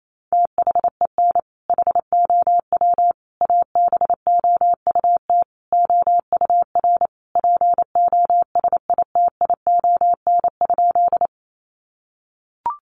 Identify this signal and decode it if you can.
Morse code, heard as a keyed tone, about 28 wpm, 690 Hz. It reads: THEN HOW ABOUT OUR POSITION?
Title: Sentences from Top 900 Words 28wpm